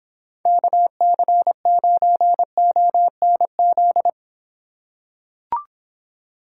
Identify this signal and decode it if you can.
Morse code, heard as a keyed tone, about 26 wpm, 700 Hz. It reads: KC9ONZ